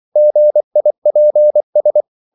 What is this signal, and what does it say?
Morse code, heard as a keyed tone, about 24 wpm, 600 Hz. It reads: GIPS
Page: Korte danske ord 24wpm